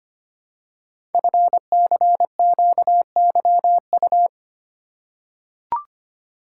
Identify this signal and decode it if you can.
Morse code, heard as a keyed tone, about 25 wpm, 700 Hz. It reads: FCQYU
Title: Code Groups - 5 Letters - 25wpm